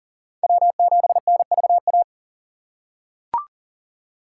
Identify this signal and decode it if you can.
Morse code, heard as a keyed tone, about 40 wpm, 700 Hz. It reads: W7NVA
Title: USA Call Signs - 5 Characters - 2x Character Spacing - 40wpm